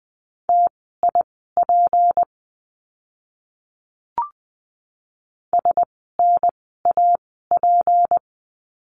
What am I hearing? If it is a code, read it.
Morse code, heard as a keyed tone, about 20 wpm, spaced as for 14 wpm, 700 Hz. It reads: TIP SNAP
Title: 153 Lesson 25f - Letter p - Words 20wpm